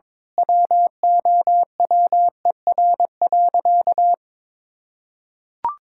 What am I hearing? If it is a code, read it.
Morse code, heard as a keyed tone, about 22 wpm, 700 Hz. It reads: WOWER.
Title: Pangram Challenge v2 - 22wpm